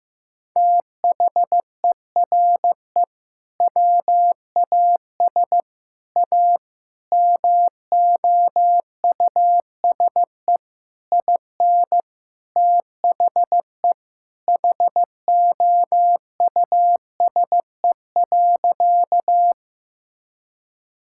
Morse code, heard as a keyed tone, about 14 wpm, 700 Hz. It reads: THERE WAS A MOUSE IN THE HOUSE.
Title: Sentences from Top 1000 Words - Review Part 5 - 15wpm